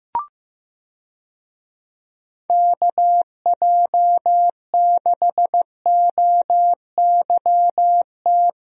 Morse code, heard as a keyed tone, about 15 wpm, 700 Hz. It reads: KJ6OYT
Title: USA Call Signs - 6 Characters - Repeated 2x - 15wpm